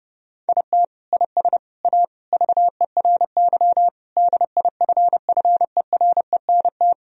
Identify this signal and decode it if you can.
Morse code, heard as a keyed tone, about 30 wpm, 700 Hz. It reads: IT IS A VERY DIFFERENT
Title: Sentences from Top 1000 Words - Review Part 4 - 30wpm